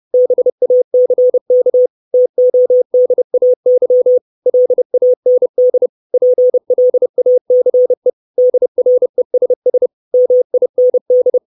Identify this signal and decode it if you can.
Morse code, heard as a keyed tone, about 30 wpm, 500 Hz. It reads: BACK TODAY LAND PLACE DRESS MIND
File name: 632 Words 30WPM